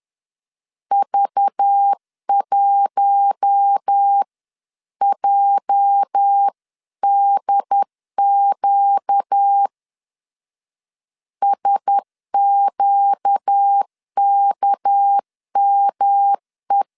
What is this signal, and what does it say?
Morse code, heard as a keyed tone, about 10 wpm, 800 Hz. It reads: V1JDQ SQKME
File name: CW / DET / 1 9 1500C S A 10wpm 4511